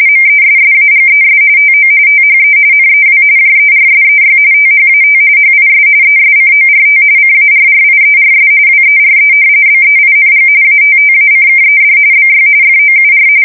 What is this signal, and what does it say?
RTTY, about 45.45 baud. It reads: MESSAGES TO THE NEW JERSEY THAT HIS ESCORT CARRIERS WERE BEING ATTACKED BY FOUR
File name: fsk170 / battleshipnj